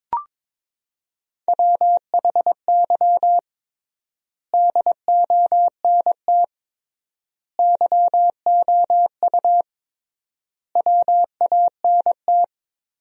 Morse code, heard as a keyed tone, about 22 wpm, 700 Hz. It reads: WHY DONT YOU WANT
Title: 3x Word-Spacing - Sentences from Top 300 Words - 22wpm